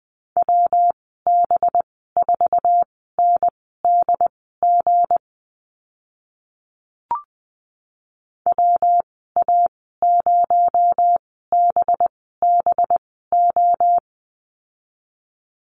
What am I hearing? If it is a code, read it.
Morse code, heard as a keyed tone, about 20 wpm, spaced as for 15 wpm, 700 Hz. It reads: WB4NDG WA0BBO
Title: 238 Lesson 38g - Letter q - Call Signs 20wpm